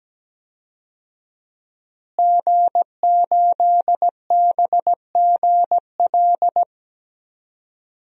Morse code, heard as a keyed tone, about 17 wpm, 700 Hz. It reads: G8BGL
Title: DXCC-100 Call Signs - 5 Characters - Repeated 3x - 17wpm